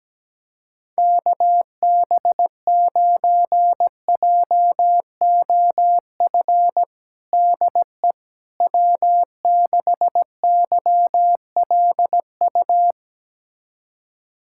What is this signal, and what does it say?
Morse code, heard as a keyed tone, about 17 wpm, 700 Hz. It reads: KB9JOF DE W6YLU
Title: USA Callsign DE Callsign 17wpm